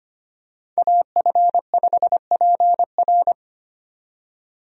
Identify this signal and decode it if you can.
Morse code, heard as a keyed tone, about 25 wpm, 700 Hz. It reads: AF5PR
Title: USA Call Signs - 5 Characters - 25wpm